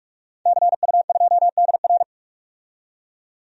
Morse code, heard as a keyed tone, about 44 wpm, 700 Hz. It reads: CU2DR
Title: DXCC-100 Call Signs - 5 Characters - 45wpm